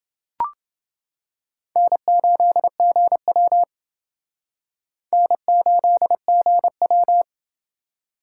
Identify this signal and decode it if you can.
Morse code, heard as a keyed tone, about 28 wpm, 700 Hz. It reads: N8GW N8GW